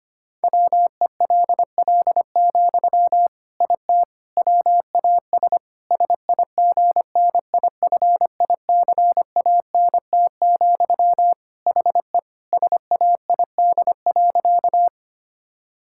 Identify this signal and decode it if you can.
Morse code, heard as a keyed tone, about 25 wpm, 700 Hz. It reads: WELL, IT WAS SIGNIFICANT, HE SAID.
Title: Sentences from Top 1000 Words - Review Part 6 - 25wpm